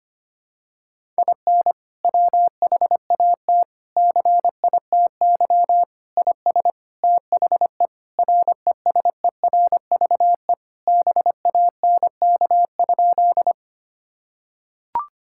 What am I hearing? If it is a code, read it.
Morse code, heard as a keyed tone, about 25 wpm, 700 Hz. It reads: IN WHAT CITY IS THE RESERVE BANK?
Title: Sentences from Top 1000 Words - Review Part 6 - 25wpm